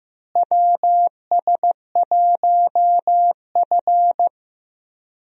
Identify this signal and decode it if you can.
Morse code, heard as a keyed tone, about 16 wpm, 700 Hz. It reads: WS1F
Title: USA Call Signs - 4 Characters - Repeated 2x - 15wpm